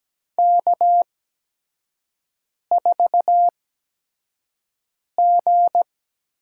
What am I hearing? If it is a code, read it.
Morse code, heard as a keyed tone, about 17 wpm, spaced as for 5 wpm, 700 Hz. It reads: K4G
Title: USA Call Signs - 4 Characters - 8x Character Spacing - 17wpm